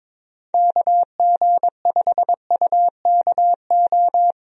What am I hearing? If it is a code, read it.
Morse code, heard as a keyed tone, about 22 wpm, 700 Hz. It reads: KG5UKO
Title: USA Call Signs - 6 Characters - Repeated 3x - 22wpm